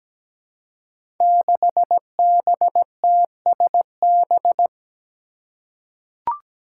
Morse code, heard as a keyed tone, about 17 wpm, 700 Hz. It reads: 6BTSB
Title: Five Challenging Letter-Number Combinations 17wpm